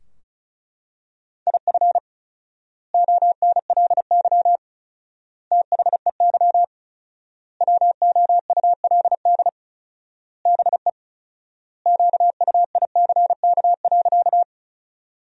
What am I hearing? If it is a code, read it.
Morse code, heard as a keyed tone, about 36 wpm, 700 Hz. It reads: IF ONLY THEY WOULD BE QUICK.
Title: 4x Word-Spacing - Sentences from Top 500 Words - 35wpm